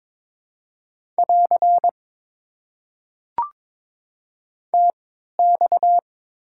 Morse code, heard as a keyed tone, about 22 wpm, spaced as for 12 wpm, 700 Hz. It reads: <AR> TX
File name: Top 90 QSO Elements - 3x Character Spacing - 22wpm